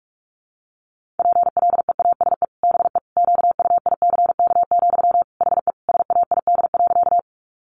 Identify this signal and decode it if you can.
Morse code, heard as a keyed tone, about 44 wpm, 700 Hz. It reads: PLEASE BE QUICK, HE SAID.
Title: Sentences from Top 1500 Words 45wpm